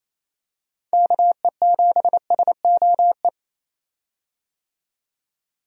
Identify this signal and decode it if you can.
Morse code, heard as a keyed tone, about 27 wpm, 700 Hz. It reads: KE7SOE
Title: USA Call Signs - 6 Characters - 28wpm